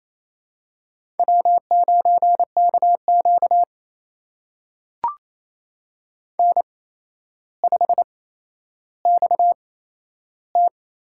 Morse code, heard as a keyed tone, about 28 wpm, spaced as for 12 wpm, 700 Hz. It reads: W9KQ N5XT
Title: USA Call Signs - 8x Character Spacing - 28wpm